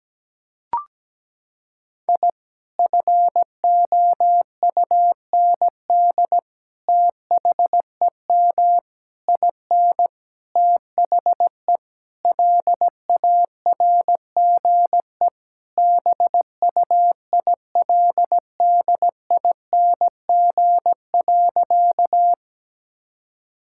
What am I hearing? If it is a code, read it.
Morse code, heard as a keyed tone, about 17 wpm, 700 Hz. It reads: I FOUND THEM IN THE LARGE BUILDING.
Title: Sentences from Top 1000 Words - Review Part 5 - 17wpm